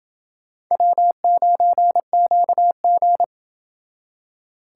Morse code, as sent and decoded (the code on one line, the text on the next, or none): .-- ----. --.- --.
W9QG